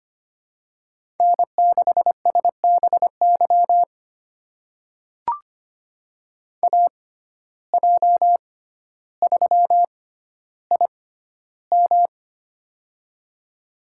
Morse code, heard as a keyed tone, about 25 wpm, spaced as for 13 wpm, 690 Hz. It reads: N6SBY AJ3IM
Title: USA Call Signs - 5 Characters - 6x Character Spacing - 25wpm